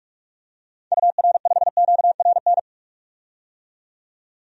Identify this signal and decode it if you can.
Morse code, heard as a keyed tone, about 46 wpm, 700 Hz. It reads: UR5QRN
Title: DXCC-100 Call Signs - 6 Characters - 45wpm